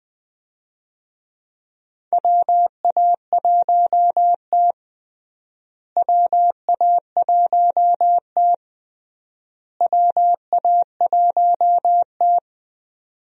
Morse code, heard as a keyed tone, about 20 wpm, 700 Hz. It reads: WA1T WA1T WA1T
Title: USA Call Signs - 4 Characters - Repeated 3x - 20wpm